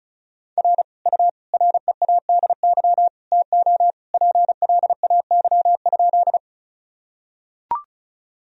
Morse code, heard as a keyed tone, about 34 wpm, 700 Hz. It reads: R U READY TO PLAY?
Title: Sentences from Top 400 Words and Abbreviations 35wpm